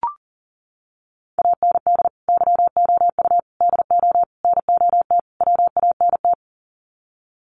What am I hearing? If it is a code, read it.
Morse code, heard as a keyed tone, about 40 wpm, 700 Hz. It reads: AND YOU DO NOT WANT
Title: Sets of 5 Words Encore2 40wpm